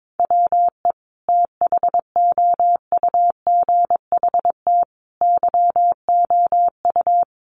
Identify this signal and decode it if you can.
Morse code, heard as a keyed tone, about 22 wpm, 700 Hz. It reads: WE THOUGHT YOU
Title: Sentences from Top 800 Words 22wpm